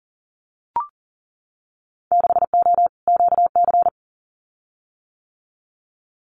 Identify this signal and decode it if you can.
Morse code, heard as a keyed tone, about 42 wpm, 700 Hz. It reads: 6O QC
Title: Winter Field Day Exchanges Encore 40wpm